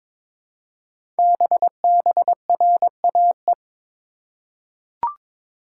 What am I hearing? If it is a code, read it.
Morse code, heard as a keyed tone, about 22 wpm, 700 Hz. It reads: BBRAE
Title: Code Groups - 5 Letters - 22wpm